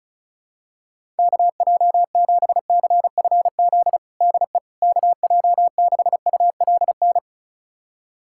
Morse code, heard as a keyed tone, about 34 wpm, 700 Hz. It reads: KJ7CFZ DE KJ6ULN
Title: USA Callsign DE Callsign - 6 Characters - 35wpm